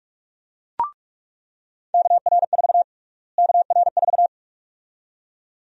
Morse code, heard as a keyed tone, about 46 wpm, 700 Hz. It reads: KR4 KR4
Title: USA Call Sign Prefixes - Repeated 2x - 45wpm